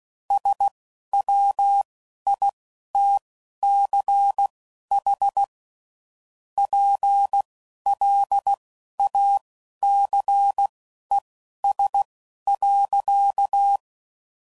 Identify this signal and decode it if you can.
Morse code, heard as a keyed tone, about 16 wpm, spaced as for 11 wpm, 800 Hz. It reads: SWITCH PLACES.